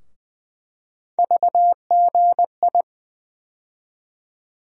Morse code, heard as a keyed tone, about 20 wpm, 700 Hz. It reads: VGI
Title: USA Call Sign Postfixes - 3 Letters 20wpm